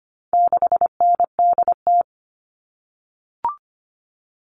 Morse code, heard as a keyed tone, about 25 wpm, 700 Hz. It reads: 6NDT
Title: All Cut Numbers - 4-Digit 25wpm